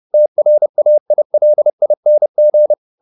Morse code, heard as a keyed tone, about 30 wpm, 600 Hz. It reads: TRAILING
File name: the old man and the sea / the old man and the sea 1 30wpm